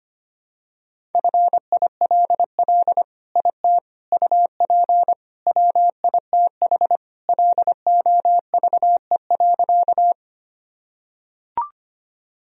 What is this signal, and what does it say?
Morse code, heard as a keyed tone, about 25 wpm, 700 Hz. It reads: FILL IT UP WITH LOVE.
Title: Sentences from Top 1000 Words - Review Part 3 - 25wpm